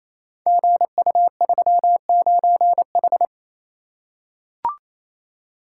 Morse code, heard as a keyed tone, about 28 wpm, 700 Hz. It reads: GU39H
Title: Code Groups - 5 Letter:Numbers - 28wpm